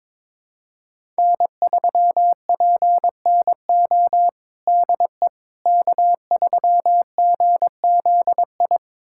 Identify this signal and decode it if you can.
Morse code, heard as a keyed tone, about 22 wpm, 700 Hz. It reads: N3PNO DE K3GZI
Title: USA Callsign DE Callsign - 5 Characters - 22wpm